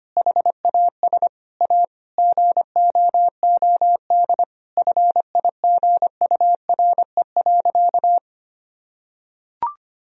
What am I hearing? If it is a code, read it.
Morse code, heard as a keyed tone, about 25 wpm, 700 Hz. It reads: HAS A GOOD FIGURE.
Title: Sentences from Top 1000 Words - Review Part 2 - 25wpm